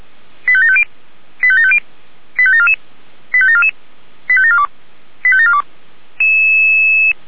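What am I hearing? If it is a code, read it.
Five-tone selective call (ZVEI), heard as quick runs of five tones, tones of about 70 ms. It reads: 85750 85750 85744 85744 85742 85742
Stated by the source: ZVEI five-tone call 85749 received by SDR, logged by Feuerwehr Orlamünde fire brigade.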